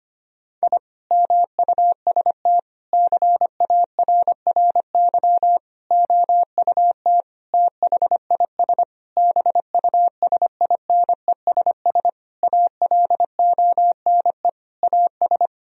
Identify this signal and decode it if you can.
Morse code, heard as a keyed tone, about 25 wpm, 700 Hz. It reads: I MUST CARRY OUT THIS BUSINESS ALONE AS
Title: Sentences from Top 1000 Words - Review Part 2 - 25wpm